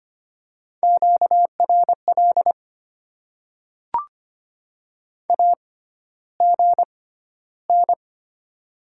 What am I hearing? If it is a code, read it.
Morse code, heard as a keyed tone, about 25 wpm, spaced as for 13 wpm, 690 Hz. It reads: QRL AGN
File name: Top 90 QSO Elements - 6x Character Spacing - 25wpm